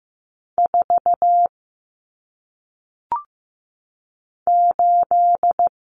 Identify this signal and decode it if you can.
Morse code, heard as a keyed tone, about 15 wpm, 700 Hz. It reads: 4 8
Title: Numbers - 1-Digit 15wpm